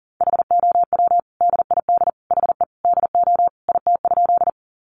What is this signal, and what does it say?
Morse code, heard as a keyed tone, about 40 wpm, 700 Hz. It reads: HOW DID HE DO IT?
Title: Sentences from Top 100 Words 40wpm